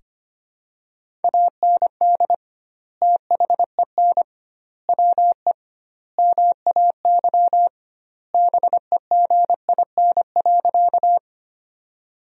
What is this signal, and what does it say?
Morse code, heard as a keyed tone, about 25 wpm, 700 Hz. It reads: AND THEN WE MAY BEGIN.
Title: 2x Word-Spacing - Sentences from Top 300 Words - 25wpm